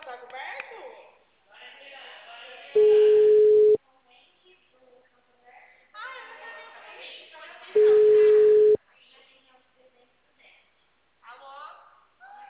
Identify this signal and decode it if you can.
Ringback tone (Europe, 425 Hz), with speech.
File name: out- -2054- - - .41701